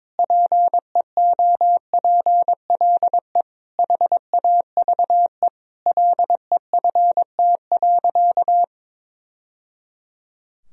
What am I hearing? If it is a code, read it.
Morse code, heard as a keyed tone, about 22 wpm, 700 Hz. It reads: PEOPLE HAVE LEFT.
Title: Sentences from Top 1000 Words - Review Part 1 - 22wpm